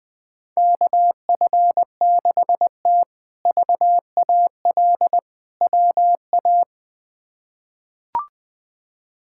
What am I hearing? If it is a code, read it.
Morse code, heard as a keyed tone, about 21 wpm, 700 Hz. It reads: KF6T VAL WA
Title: K1USN SST QSO Practice - 20wpm